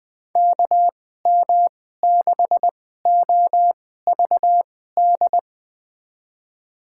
Morse code, heard as a keyed tone, about 20 wpm, spaced as for 15 wpm, 700 Hz. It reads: KM6OVD
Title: USA Call Signs - 6 Characters - 2x Character Spacing - 20wpm